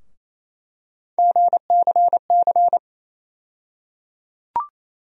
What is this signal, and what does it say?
Morse code, heard as a keyed tone, about 28 wpm, 700 Hz. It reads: GCC
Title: USA Call Sign Postfixes - 3 Letters 28wpm